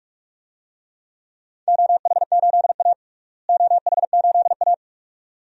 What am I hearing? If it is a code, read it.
Morse code, heard as a keyed tone, about 46 wpm, 700 Hz. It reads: OH8A OH8A
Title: DXCC-100 Call Signs - 4 Characters - Repeated 2x - 45wpm